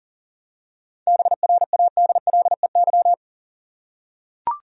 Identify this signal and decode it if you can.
Morse code, heard as a keyed tone, about 40 wpm, 700 Hz. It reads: BRADLEY
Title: USA Common Names - 40wpm